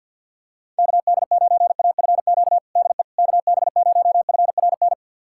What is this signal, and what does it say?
Morse code, heard as a keyed tone, about 51 wpm, 700 Hz. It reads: KD9AFQ DE KB0FRN
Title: USA Callsign DE Callsign - 6 Characters - 50wpm